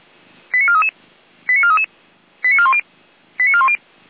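Five-tone selective call (ZVEI), heard as quick runs of five tones, tones of about 70 ms. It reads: 89433 89433 89410 89410
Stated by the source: ZVEI five-tone selective call received on SDR.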